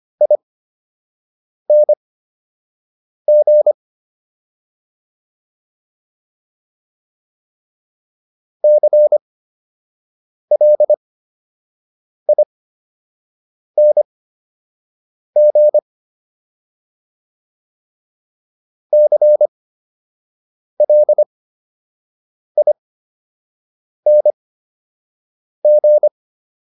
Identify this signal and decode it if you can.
Morse code, heard as a keyed tone, about 25 wpm, spaced as for 6 wpm, 600 Hz. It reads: ING CLING CLING